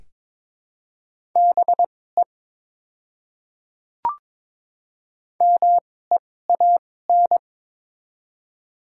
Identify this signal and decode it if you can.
Morse code, heard as a keyed tone, about 22 wpm, spaced as for 16 wpm, 700 Hz. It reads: BE MEAN
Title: 2x Character Spacing - Top 200 Words - 22wpm